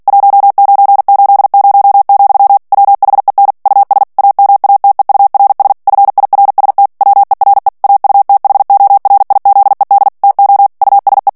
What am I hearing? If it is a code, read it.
Morse code, heard as a keyed tone, about 48 wpm, 800 Hz. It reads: 1980, WHEN US AMATEURS FIRST WERE AUTHORIZED TO USE
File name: ANGLAIS texte clair 3 vitesse 45 mots minute